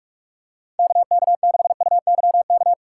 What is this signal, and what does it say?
Morse code, heard as a keyed tone, about 44 wpm, 700 Hz. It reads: KK6UYK
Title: USA Call Signs - 6 Characters - 3x Character Spacing - 45wpm